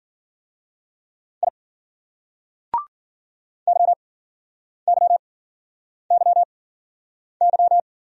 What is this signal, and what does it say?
Morse code, heard as a keyed tone, about 54 wpm, spaced as for 9 wpm, 700 Hz. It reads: I YYYY